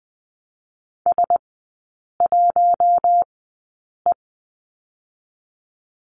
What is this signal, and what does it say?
Morse code, heard as a keyed tone, about 19 wpm, spaced as for 8 wpm, 700 Hz. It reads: S1E
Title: 060 Lesson 11e - Letter h - Three Characters 20wpm